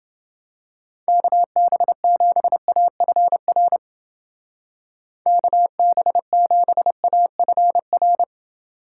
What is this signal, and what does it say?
Morse code, heard as a keyed tone, about 28 wpm, 700 Hz. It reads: KB7AFR KB7AFR